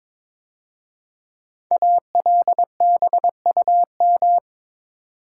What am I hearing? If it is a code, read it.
Morse code, heard as a keyed tone, about 22 wpm, 690 Hz. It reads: ALBUM
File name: 5 Letter Words - First 3 Letters Given - 22wpm